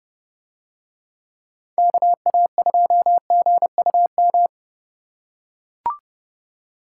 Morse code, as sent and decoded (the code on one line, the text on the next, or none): -.- .- ..--- --. ..- --
KA2GUM